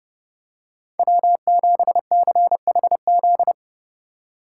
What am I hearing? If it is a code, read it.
Morse code, heard as a keyed tone, about 31 wpm, 700 Hz. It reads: W7CHZ
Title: USA Call Signs - 5 Characters - 4x Character Spacing - 30wpm